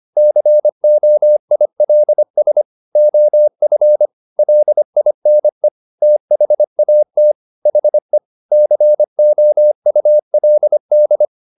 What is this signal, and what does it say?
Morse code, heard as a keyed tone, about 25 wpm, 600 Hz. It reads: COILS OF LINE THAT HE COULD